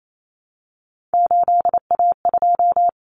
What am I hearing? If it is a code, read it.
Morse code, heard as a keyed tone, about 28 wpm, 700 Hz. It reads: 8A2
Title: Common Cut Numbers - 3-Digit 28wpm